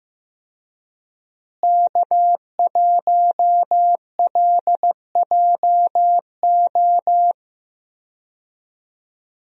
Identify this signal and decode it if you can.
Morse code, heard as a keyed tone, about 15 wpm, 700 Hz. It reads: K1LJO
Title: USA Call Signs - 5 Characters - Repeated 2x - 15wpm